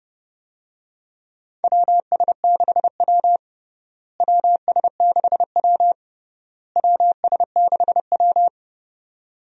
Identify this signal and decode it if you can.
Morse code, heard as a keyed tone, about 30 wpm, 700 Hz. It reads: WS6W WS6W WS6W
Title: USA Call Signs - 4 Characters - Repeated 3x - 30wpm